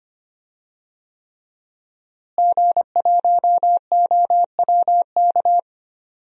Morse code, heard as a keyed tone, about 25 wpm, 700 Hz. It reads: G1OWK
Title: DXCC-100 Call Signs - 5 Characters - Repeated 3x - 25wpm